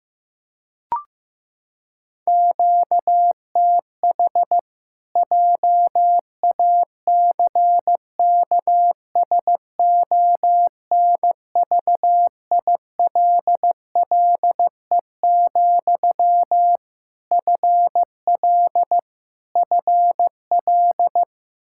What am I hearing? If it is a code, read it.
Morse code, heard as a keyed tone, about 16 wpm, 700 Hz. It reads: QTH JACKSONVILLE, FL FL
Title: Practice QSO - Exchange 1 - Beginning - 15wpm